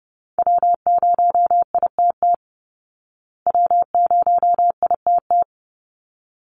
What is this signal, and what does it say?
Morse code, heard as a keyed tone, about 30 wpm, 700 Hz. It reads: W0ITT W0ITT
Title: Call Signs Repeated 3x - 30wpm